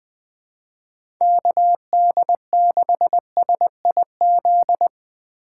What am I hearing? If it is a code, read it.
Morse code, heard as a keyed tone, about 20 wpm, 700 Hz. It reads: KD6SIZ
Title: USA Call Signs - 6 Characters - 2x Character Spacing - 20wpm